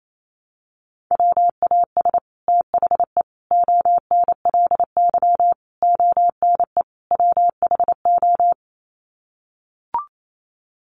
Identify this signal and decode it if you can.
Morse code, heard as a keyed tone, about 28 wpm, 700 Hz. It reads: WAS THE ONLY ONE WHO
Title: Sets of 5 Words Encore2 28wpm